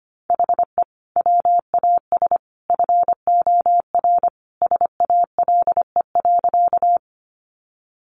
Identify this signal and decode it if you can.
Morse code, heard as a keyed tone, about 25 wpm, 700 Hz. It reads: HE WAS FOR SALE.
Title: Sentences from Top 1400 Words 25wpm